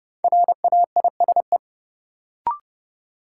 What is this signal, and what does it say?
Morse code, heard as a keyed tone, about 30 wpm, 700 Hz. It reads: RAISE
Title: Sentences from Top 1400 Words - New Word Drill Alt - 30wpm